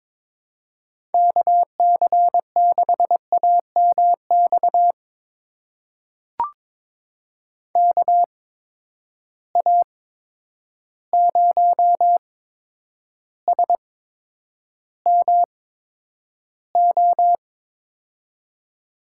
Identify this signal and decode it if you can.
Morse code, heard as a keyed tone, about 22 wpm, spaced as for 9 wpm, 700 Hz. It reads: KC6AMX KA0SMO
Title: USA Call Signs - 6 Characters - 8x Character Spacing - 22wpm